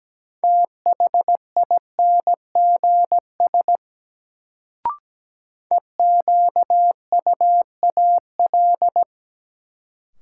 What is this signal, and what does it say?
Morse code, heard as a keyed tone, about 17 wpm, 700 Hz. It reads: THINGS EQUAL
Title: Sentences from Top 1400 Words - New Word Drill Alt - 17wpm